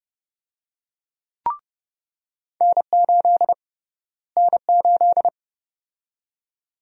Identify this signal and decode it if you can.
Morse code, heard as a keyed tone, about 30 wpm, 700 Hz. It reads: N8 N8
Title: USA Call Sign Prefixes - Repeated 2x - 30wpm